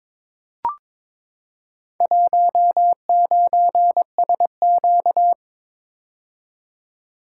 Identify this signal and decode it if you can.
Morse code, heard as a keyed tone, about 22 wpm, 700 Hz. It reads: 19SQ
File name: practice / 22wpm / Four Letter-Number Combinations - 22wpm